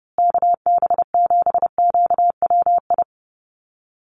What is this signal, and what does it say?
Morse code, heard as a keyed tone, about 30 wpm, 700 Hz. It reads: KB7QWI
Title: USA Call Signs Repeated 3x 30wpm